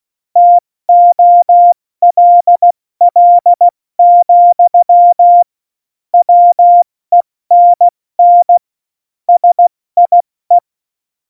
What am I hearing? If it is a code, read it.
Morse code, heard as a keyed tone, about 16 wpm, 700 Hz. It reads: TOLL, WENN SIE
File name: Morsekurs / Witze / Witz 20. Februar 2018, 70 F80